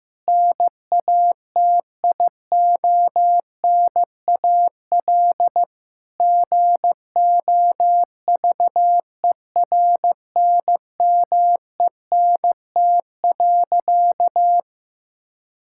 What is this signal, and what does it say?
Morse code, heard as a keyed tone, about 15 wpm, 700 Hz. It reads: NATIONAL GOVERNMENT.
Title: Sentences from Top 1000 Words - Review Part 6 - 15wpm